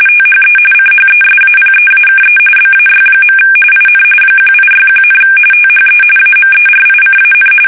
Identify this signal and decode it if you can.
RTTY, about 45.45 baud. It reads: THE CHESAPEAKE BAY. IN 1861, THE AMERICAN
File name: fsk850 / usnavy1